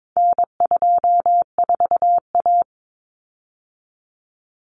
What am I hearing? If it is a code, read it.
Morse code, heard as a keyed tone, about 22 wpm, 700 Hz. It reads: N24A
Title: Common Cut Numbers - 4-Digit 22wpm